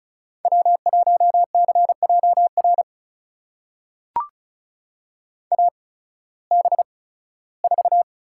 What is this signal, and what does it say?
Morse code, heard as a keyed tone, about 34 wpm, spaced as for 18 wpm, 700 Hz. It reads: W1CJR AB4
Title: USA Call Signs - 5 Characters - 8x Character Spacing - 35wpm